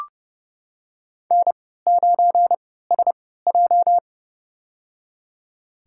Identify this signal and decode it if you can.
Morse code, heard as a keyed tone, about 31 wpm, spaced as for 17 wpm, 700 Hz. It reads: N9SJ
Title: USA Call Signs - 4 Characters - 3x Character Spacing - 30wpm